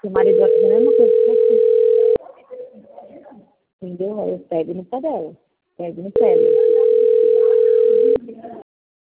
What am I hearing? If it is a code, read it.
Ringback tone (North America), 440 + 480 Hz, with speech.
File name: rg-1600-unknown- - - .16424